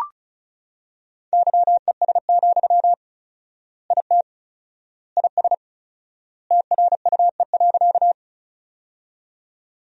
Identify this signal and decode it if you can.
Morse code, heard as a keyed tone, about 36 wpm, 700 Hz. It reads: YES, IT IS TRUE.
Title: 4x Word-Spacing - Sentences from Top 500 Words - 35wpm